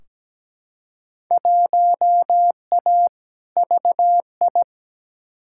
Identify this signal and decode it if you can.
Morse code, heard as a keyed tone, about 17 wpm, 700 Hz. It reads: 1A VI
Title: Field Day Exchanges Encore - 17wpm